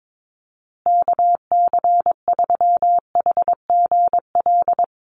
Morse code, heard as a keyed tone, about 22 wpm, 700 Hz. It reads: KC3HGL
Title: USA Call Signs Repeated 2x 22wpm